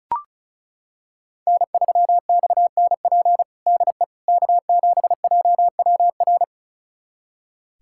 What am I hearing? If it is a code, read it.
Morse code, heard as a keyed tone, about 36 wpm, 700 Hz. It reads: N3XNP DE K7JWR
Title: USA Callsign DE Callsign - 5 Characters - 35wpm